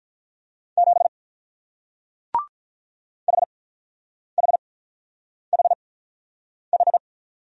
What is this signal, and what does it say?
Morse code, heard as a keyed tone, about 43 wpm, spaced as for 9 wpm, 690 Hz. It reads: 7 HHHH
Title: Speed-Racing Single Letter-Number 35wpm